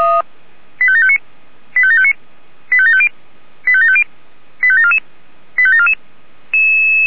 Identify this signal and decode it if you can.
Five-tone selective call (ZVEI), heard as quick runs of five tones, tones of about 70 ms. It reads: 85759 85759 85750 85750 85744 85744
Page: SDR/ZVEI